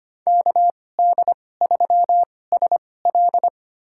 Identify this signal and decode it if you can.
Morse code, heard as a keyed tone, about 25 wpm, spaced as for 18 wpm, 700 Hz. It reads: KD3SL
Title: USA Call Signs - 5 Characters - 2x Character Spacing - 25wpm